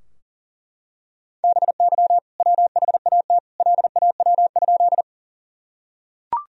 Morse code, heard as a keyed tone, about 40 wpm, 700 Hz. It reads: BY WHAT LAW?
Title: Sentences from Top 1000 Words - Review Part 3 - 40wpm